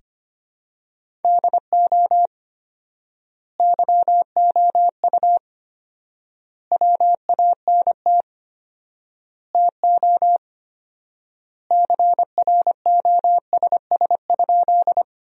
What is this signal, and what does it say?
Morse code, heard as a keyed tone, about 25 wpm, 700 Hz. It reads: DO YOU WANT TO CROSS?
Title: 4x Word-Spacing - Sentences from Top 300 Words - 25wpm